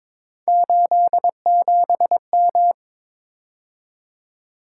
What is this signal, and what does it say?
Morse code, heard as a keyed tone, about 27 wpm, 700 Hz. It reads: 87M
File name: Speed-Racing - Three Letter-Number Combinations - 22wpm